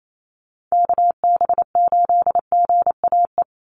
Morse code, heard as a keyed tone, about 28 wpm, 700 Hz. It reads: KB8GAE
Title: Call Signs - 28wpm